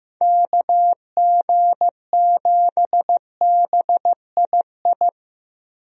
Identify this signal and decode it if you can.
Morse code, heard as a keyed tone, about 15 wpm, 700 Hz. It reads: KG7BII
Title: USA Call Signs - 6 Characters - Repeated 2x - 15wpm